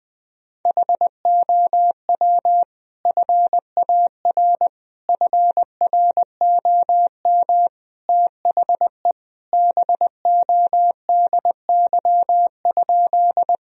Morse code, heard as a keyed tone, about 21 wpm, 700 Hz. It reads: HOW FAR FROM THE BODY?
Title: Sentences from Top 1000 Words - Review Part 2 - 20wpm